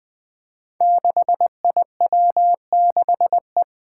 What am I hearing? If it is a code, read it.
Morse code, heard as a keyed tone, about 20 wpm, 700 Hz. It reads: 6IW6E
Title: Five Challenging Letter-Number Combinations 20wpm